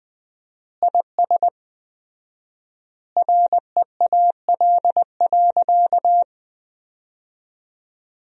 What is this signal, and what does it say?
Morse code, heard as a keyed tone, about 21 wpm, 700 Hz. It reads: IS REAL.
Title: 4x Word-Spacing - Sentences from Top 300 Words - 20wpm